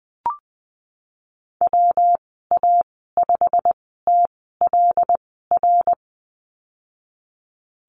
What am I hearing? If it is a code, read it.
Morse code, heard as a keyed tone, about 20 wpm, spaced as for 15 wpm, 700 Hz. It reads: WA5TLR
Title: 154 Lesson 25g - Letter p - Call Signs 20wpm